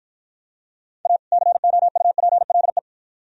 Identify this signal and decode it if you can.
Morse code, heard as a keyed tone, about 53 wpm, 700 Hz. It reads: A COUPLE